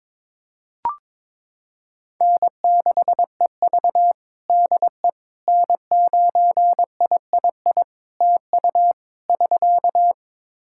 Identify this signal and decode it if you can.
Morse code, heard as a keyed tone, about 22 wpm, 700 Hz. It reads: N6EV DE N9III TU <SK>
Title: Practice QSO - Exchange 4 - Ending - 22wpm